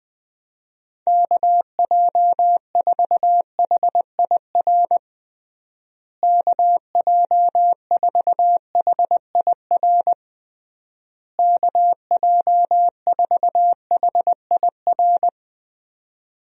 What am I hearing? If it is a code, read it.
Morse code, heard as a keyed tone, about 21 wpm, 700 Hz. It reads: KJ4HIR KJ4HIR KJ4HIR
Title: USA Call Signs - 6 Characters - Repeated 3x - 20wpm